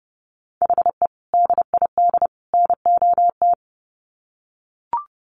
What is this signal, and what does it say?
Morse code, heard as a keyed tone, about 30 wpm, 700 Hz. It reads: HE DID NOT
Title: Sets of 3 Words 30wpm